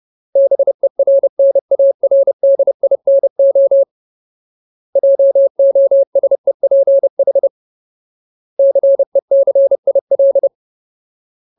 Morse code, heard as a keyed tone, about 30 wpm, 550 Hz. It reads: BERNARDINO JOSEPH CECIL
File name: Ultimate Word List / 30WPM Ultimate